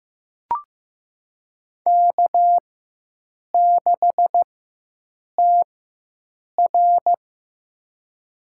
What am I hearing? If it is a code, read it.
Morse code, heard as a keyed tone, about 15 wpm, spaced as for 7 wpm, 700 Hz. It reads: K6TR
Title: USA Call Signs - 4 Characters - 4x Character Spacing - 15wpm